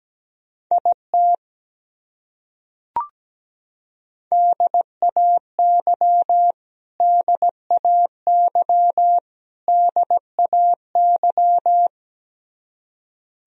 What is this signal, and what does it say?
Morse code, heard as a keyed tone, about 17 wpm, 700 Hz. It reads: IT DAY DAY DAY
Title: Top 100 Words Repeated 17wpm